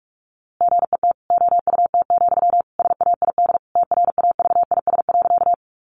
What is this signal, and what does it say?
Morse code, heard as a keyed tone, about 44 wpm, 700 Hz. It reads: GET OUT, SAID TRAVIS.
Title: A Sound Of Thunder 45wpm